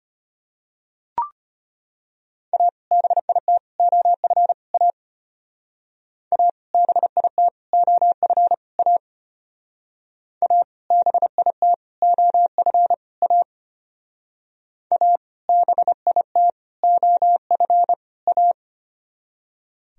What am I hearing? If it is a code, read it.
Morse code, heard as a keyed tone, about 33 wpm, 700 Hz. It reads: A BIT OF A A BIT OF A A BIT OF A A BIT OF A